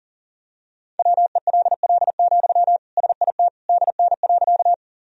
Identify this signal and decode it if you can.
Morse code, heard as a keyed tone, about 40 wpm, 700 Hz. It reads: WELL, SIT DN.
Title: Sentences from Top 500 Words and Abbreviations 40wpm